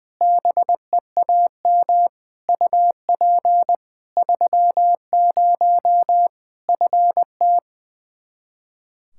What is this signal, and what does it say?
Morse code, heard as a keyed tone, about 20 wpm, 700 Hz. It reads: BEAM UP 30 FT
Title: Practice QSO - Exchange 2 - Info - 20wpm